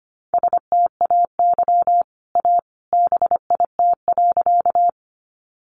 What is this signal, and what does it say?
Morse code, heard as a keyed tone, about 25 wpm, 700 Hz. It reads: STAY A BIT.
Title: Sentences from Top 900 Words 25wpm